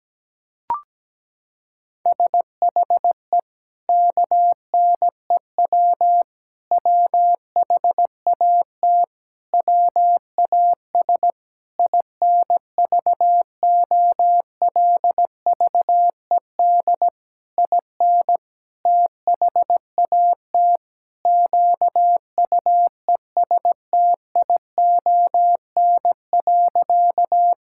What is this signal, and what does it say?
Morse code, heard as a keyed tone, about 17 wpm, 700 Hz. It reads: SHE KNEW WHAT WAS INVOLVED IN THAT QUESTION.
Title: Sentences from Top 1000 Words - Review Part 5 - 17wpm